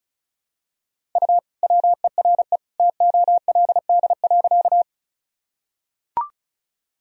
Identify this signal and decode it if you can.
Morse code, heard as a keyed tone, about 36 wpm, 700 Hz. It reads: U WERE TOLD.
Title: Sentences from Top 400 Words and Abbreviations 35wpm